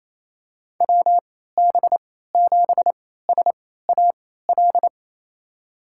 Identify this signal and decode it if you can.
Morse code, heard as a keyed tone, about 28 wpm, spaced as for 16 wpm, 700 Hz. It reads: WB7SAL
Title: USA Call Signs - 3x Character Spacing - 28wpm